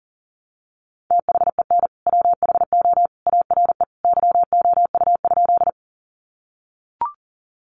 Morse code, heard as a keyed tone, about 40 wpm, 700 Hz. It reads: THEN WHO ARE YOU?
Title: Sentences from Top 100 Words 40wpm